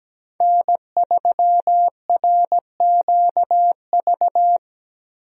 Morse code, heard as a keyed tone, about 17 wpm, 700 Hz. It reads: N3RQV
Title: USA Call Signs - 5 Characters - Repeated 3x - 17wpm